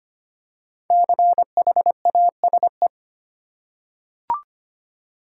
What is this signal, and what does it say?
Morse code, heard as a keyed tone, about 25 wpm, 700 Hz. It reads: CHASE
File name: 5 Letter Words - First 3 Letters Given - 25wpm